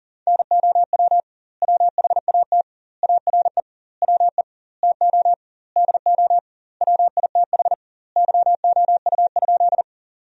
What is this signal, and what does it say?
Morse code, heard as a keyed tone, about 40 wpm, 700 Hz. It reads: NOW WHAT ARE WE TO DO WITH YOU?
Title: 2x Word-Spacing - Sentences from Top 100 Words - 40wpm